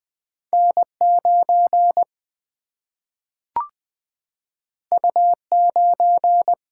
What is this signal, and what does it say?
Morse code, heard as a keyed tone, about 20 wpm, 700 Hz. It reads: N9 U9
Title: Single-Letter Single-Number 20wpm